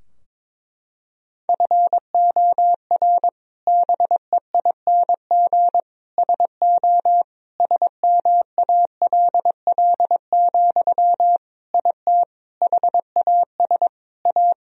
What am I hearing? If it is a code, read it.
Morse code, heard as a keyed tone, about 22 wpm, 700 Hz. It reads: FOR BEING SO SMALL, IT HAS A
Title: Sentences from Top 1000 Words - Review Part 3 - 22wpm